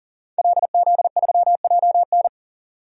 Morse code, heard as a keyed tone, about 39 wpm, 700 Hz. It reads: LZ3JN
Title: DXCC-100 Call Signs - 5 Characters - 40wpm